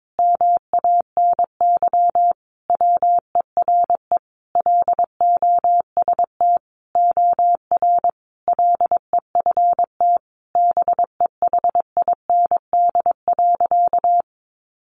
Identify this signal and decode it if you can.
Morse code, heard as a keyed tone, about 22 wpm, 700 Hz. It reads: MANY WERE LOST OR LEFT BEHIND.
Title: Sentences from Top 1400 Words 22wpm